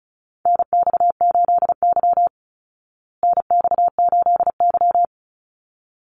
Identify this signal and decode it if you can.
Morse code, heard as a keyed tone, about 34 wpm, 700 Hz. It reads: NX8Y NX8Y
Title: Call Signs Repeated 3x - 35wpm